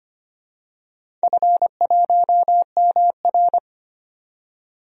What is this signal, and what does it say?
Morse code, heard as a keyed tone, about 25 wpm, 700 Hz. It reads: F1MR
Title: DXCC-100 Call Signs - 4 Characters - 25wpm